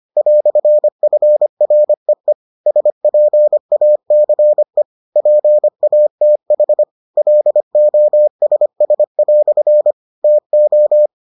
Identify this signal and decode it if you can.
Morse code, heard as a keyed tone, about 25 wpm, 600 Hz. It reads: "FREE SPACE PATH LOSS" TO